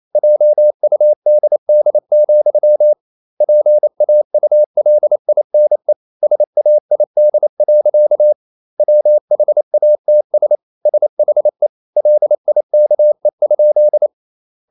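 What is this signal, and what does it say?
Morse code, heard as a keyed tone, about 28 wpm, 600 Hz. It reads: JUDD, PAULINE SAID. WHATS SHE LIKE?
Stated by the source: Morse code practice at 28 wpm, text of Ken Follett's novel Never in English, from CW med Gnister.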